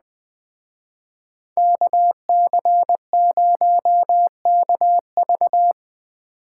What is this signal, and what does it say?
Morse code, heard as a keyed tone, about 20 wpm, 700 Hz. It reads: KC0KV
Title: Speed Racing - USA 2x2 Call Signs 17wpm